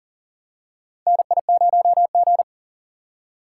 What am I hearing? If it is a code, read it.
Morse code, heard as a keyed tone, about 40 wpm, 700 Hz. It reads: NI0G
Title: USA Call Signs - 4 Characters - 8x Character Spacing - 40wpm